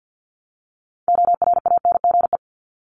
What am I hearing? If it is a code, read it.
Morse code, heard as a keyed tone, about 50 wpm, 700 Hz. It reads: ORANGE
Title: Top 1751-2000 Words 50wpm